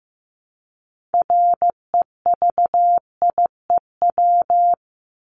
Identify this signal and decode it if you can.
Morse code, heard as a keyed tone, about 15 wpm, 700 Hz. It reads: REVIEW
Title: Top 751-1000 Words 15wpm